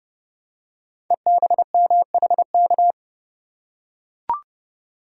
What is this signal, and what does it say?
Morse code, heard as a keyed tone, about 28 wpm, 700 Hz. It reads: EBMHK
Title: Code Groups - 5 Letters - 30wpm